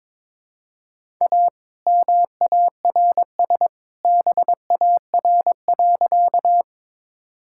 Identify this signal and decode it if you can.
Morse code, heard as a keyed tone, about 22 wpm, 700 Hz. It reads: A MARS BAR.
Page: morse-code-ninja-practice-22wpm